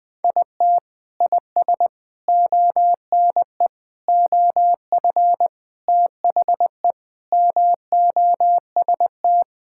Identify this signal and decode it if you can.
Morse code, heard as a keyed tone, about 20 wpm, 700 Hz. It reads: IT IS ONE OF THE MOST
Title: Sentences with 3-Word Phrases - Spoken Beginning - 20wpm